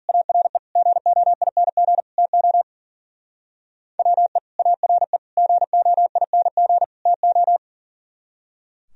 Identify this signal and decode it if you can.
Morse code, heard as a keyed tone, about 51 wpm, 700 Hz. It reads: ARE GOING TO WE ARE GOING TO